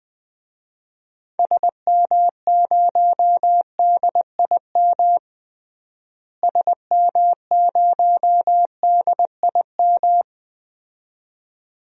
Morse code, heard as a keyed tone, about 20 wpm, 700 Hz. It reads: SM0DIM SM0DIM
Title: DXCC-100 Call Signs - 6 Characters - Repeated 2x - 20wpm